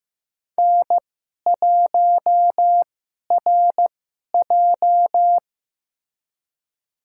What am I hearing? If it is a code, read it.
Morse code, heard as a keyed tone, about 15 wpm, spaced as for 11 wpm, 700 Hz. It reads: N1RJ
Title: USA Call Signs - 4 Characters - 2x Character Spacing - 15wpm